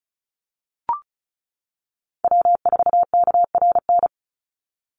Morse code, heard as a keyed tone, about 34 wpm, 700 Hz. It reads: W4KRN
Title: NA SOTA Chaser Callsigns 35wpm